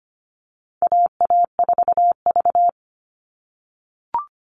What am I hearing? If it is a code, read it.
Morse code, heard as a keyed tone, about 25 wpm, 700 Hz. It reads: AA4V
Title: All Cut Numbers - 4-Digit 25wpm